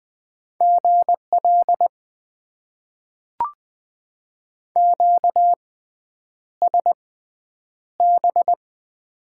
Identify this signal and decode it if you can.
Morse code, heard as a keyed tone, about 20 wpm, spaced as for 9 wpm, 700 Hz. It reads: GL QSB
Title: Top 90 QSO Elements - 6x Character Spacing - 20wpm